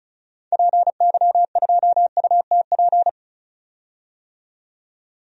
Morse code, heard as a keyed tone, about 36 wpm, 700 Hz. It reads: PY2UTP
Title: DXCC-100 Call Signs - 6 Characters - 35wpm